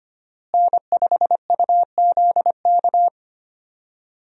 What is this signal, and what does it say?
Morse code, heard as a keyed tone, about 25 wpm, 700 Hz. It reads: N5UZK
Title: USA Call Signs - 5 Characters - Repeated 2x - 25wpm